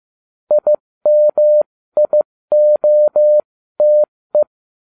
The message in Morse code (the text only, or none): IMIOTE